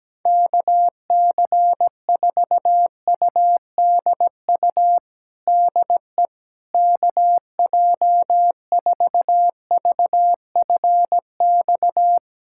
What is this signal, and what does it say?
Morse code, heard as a keyed tone, about 17 wpm, 700 Hz. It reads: KC4UDU DE KJ4VFX
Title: USA Callsign DE Callsign - 6 Characters - 17wpm